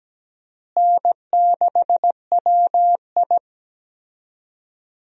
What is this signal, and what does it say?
Morse code, heard as a keyed tone, about 17 wpm, 700 Hz. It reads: N6WI
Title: USA Call Signs - 4 Characters - Repeated 3x - 17wpm